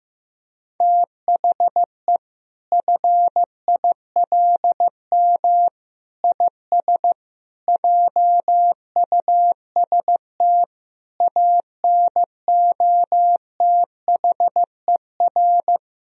Morse code, heard as a keyed tone, about 16 wpm, 700 Hz. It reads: THE FILM IS JUST ANOTHER
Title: Sentences from Top 1000 Words - Review Part 3 - 15wpm